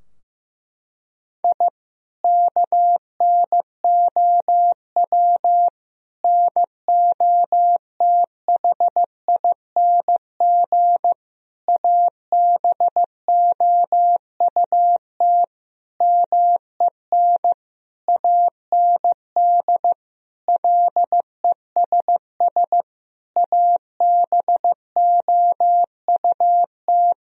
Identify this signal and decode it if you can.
Morse code, heard as a keyed tone, about 15 wpm, 700 Hz. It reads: I KNOW NOTHING ABOUT MEN AND LESS ABOUT
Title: Sentences from Top 1000 Words - Review Part 2 - 15wpm